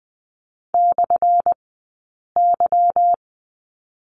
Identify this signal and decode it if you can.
Morse code, heard as a keyed tone, about 20 wpm, spaced as for 8 wpm, 690 Hz. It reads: /Y
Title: 197 Lesson 32d - Letter k - Two Characters 20wpm